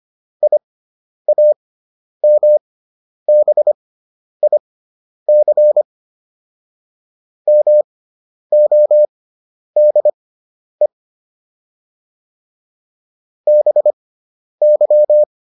Morse code, heard as a keyed tone, about 25 wpm, spaced as for 10 wpm, 600 Hz. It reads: IAMBIC MODE BY